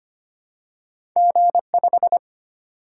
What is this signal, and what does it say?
Morse code, heard as a keyed tone, about 25 wpm, 700 Hz. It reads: G5
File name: Two Letter-Number Combinations - 25wpm